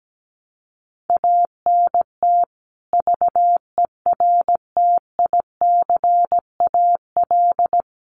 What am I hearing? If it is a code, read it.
Morse code, heard as a keyed tone, about 17 wpm, 700 Hz. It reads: ANT VERTICAL
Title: Common QSO Phrases 17wpm